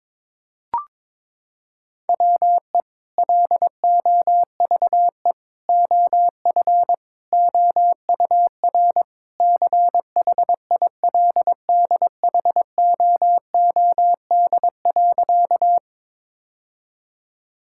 Morse code, heard as a keyed tone, about 21 wpm, 700 Hz. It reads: WE LOVE OF OUR CHILDHOOD.
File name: practice / 22wpm / Sentences with 2-Word Phrases - 22wpm